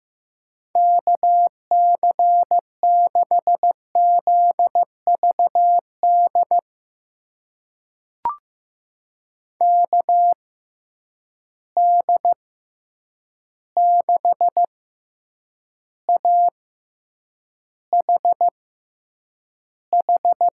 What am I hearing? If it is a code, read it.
Morse code, heard as a keyed tone, about 15 wpm, spaced as for 8 wpm, 700 Hz. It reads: KC6ZVD KD6AHH
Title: USA Call Signs - 6 Characters - 6x Character Spacing - 15wpm